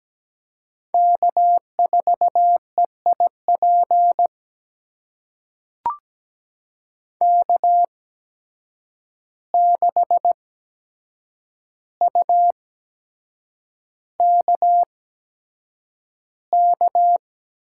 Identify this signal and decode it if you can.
Morse code, heard as a keyed tone, about 17 wpm, spaced as for 7 wpm, 700 Hz. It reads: K4EIP K6UKK
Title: USA Call Signs - 5 Characters - 8x Character Spacing - 17wpm